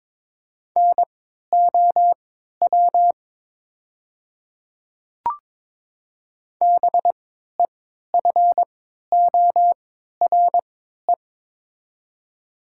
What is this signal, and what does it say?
Morse code, heard as a keyed tone, about 22 wpm, spaced as for 12 wpm, 700 Hz. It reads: NOW BEFORE
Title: 3x Character Spacing - Top 200 Words - 22wpm